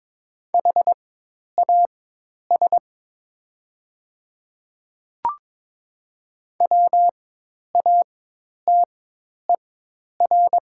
Morse code, heard as a keyed tone, about 22 wpm, spaced as for 10 wpm, 690 Hz. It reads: HAS WATER
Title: 4x Character Spacing - Top 100 Words - 22wpm